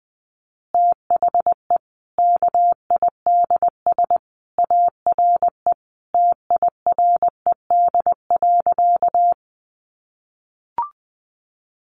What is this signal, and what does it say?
Morse code, heard as a keyed tone, about 20 wpm, 700 Hz. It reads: THE KIDS ARE TIRED.
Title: Easy Phrases 20wpm